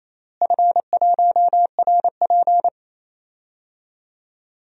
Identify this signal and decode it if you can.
Morse code, heard as a keyed tone, about 28 wpm, 700 Hz. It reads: F1RP
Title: DXCC-100 Call Signs - 4 Characters - Repeated 3x - 28wpm